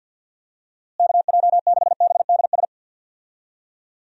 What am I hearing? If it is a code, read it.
Morse code, heard as a keyed tone, about 51 wpm, 700 Hz. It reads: KJ6BDS
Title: USA Call Signs - 6 Characters - Repeated 2x - 50wpm